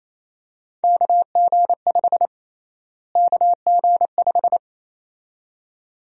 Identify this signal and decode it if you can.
Morse code, heard as a keyed tone, about 28 wpm, 700 Hz. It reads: KG5 KG5
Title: USA Call Sign Prefixes - Repeated 2x - 28wpm